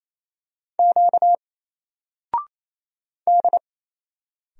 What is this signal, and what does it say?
Morse code, heard as a keyed tone, about 27 wpm, 700 Hz. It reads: Q D
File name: Single-Letter-Number-28wpm